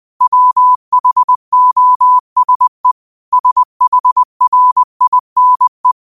Morse code, heard as a keyed tone, about 20 wpm, 1000 Hz. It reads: WHOSE SHRINE